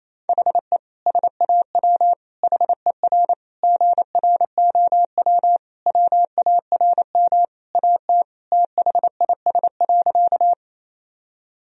Morse code, heard as a keyed tone, about 28 wpm, 700 Hz. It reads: HE SAW HER GROW WARM AT THIS.
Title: Sentences from Top 1000 Words - Review Part 2 - 28wpm